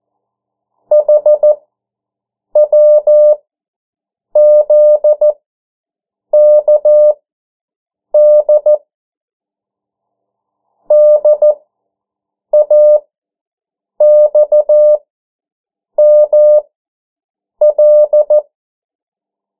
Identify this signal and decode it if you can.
Morse code, heard as a keyed tone, about 14 wpm, spaced as for 7 wpm, 600 Hz. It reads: HWZKD DAXML
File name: morse mp3 / 09 urok 8